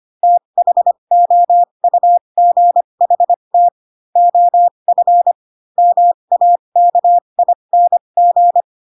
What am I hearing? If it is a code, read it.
Morse code, heard as a keyed tone, about 25 wpm, spaced as for 22 wpm, 700 Hz. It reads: THOUGHT OF MAKING
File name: Pinocchio / Pinocchio ch2 18 25 ik1hge